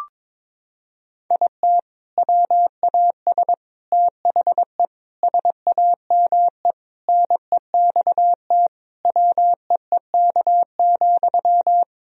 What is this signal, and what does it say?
Morse code, heard as a keyed tone, about 22 wpm, 700 Hz. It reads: IT WAS THE SAME NEXT WEEK,
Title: Sentences from Top 500 Words and Abbreviations 22wpm